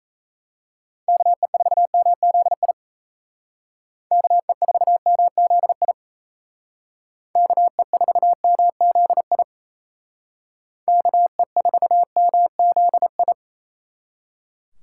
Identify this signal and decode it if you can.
Morse code, heard as a keyed tone, about 35 wpm, 700 Hz. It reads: KE4MZI KE4MZI KE4MZI KE4MZI
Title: Speed Racing - USA 2x3 Call Signs 28wpm